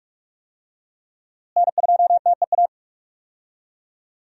Morse code, heard as a keyed tone, about 44 wpm, 700 Hz. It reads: N2TEA
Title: USA Call Signs - 5 Characters - Repeated 2x - 45wpm